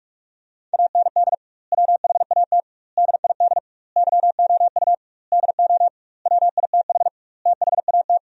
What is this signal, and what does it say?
Morse code, heard as a keyed tone, about 42 wpm, 700 Hz. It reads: AND WHAT DID YOU DO WITH THAT
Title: 2x Word-Spacing - Sentences from Top 300 Words - 45wpm